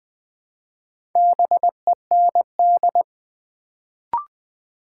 Morse code, heard as a keyed tone, about 20 wpm, 700 Hz. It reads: BEND
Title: Build A Word - 1 Letter At A Time - 3 to 5-Letter Words - 20wpm